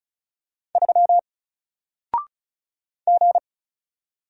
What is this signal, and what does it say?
Morse code, heard as a keyed tone, about 36 wpm, 700 Hz. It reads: 3 G
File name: Single-Letter-Number-35wpm